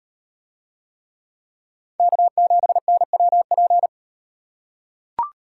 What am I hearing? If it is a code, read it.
Morse code, heard as a keyed tone, about 38 wpm, 700 Hz. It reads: K7NWP